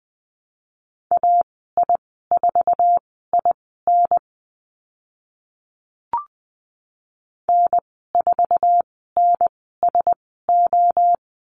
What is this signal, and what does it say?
Morse code, heard as a keyed tone, about 20 wpm, spaced as for 14 wpm, 700 Hz. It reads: AI4IN N4NSO
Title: 048 Lesson 9g - Number 4 - Call Signs 20wpm